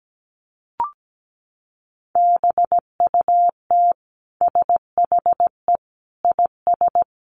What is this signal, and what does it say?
Morse code, heard as a keyed tone, about 17 wpm, 700 Hz. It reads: BUT SHE IS
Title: Sentences from Top 500 Words 17wpm